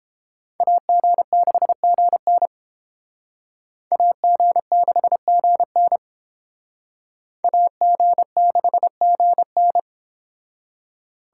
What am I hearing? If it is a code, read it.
Morse code, heard as a keyed tone, about 30 wpm, 700 Hz. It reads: AG6GN AG6GN AG6GN